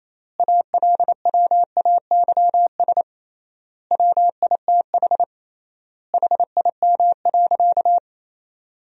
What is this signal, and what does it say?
Morse code, heard as a keyed tone, about 28 wpm, 700 Hz. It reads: ALWAYS WITH HIM.
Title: 3x Word-Spacing - Sentences from Top 400 Words - 28wpm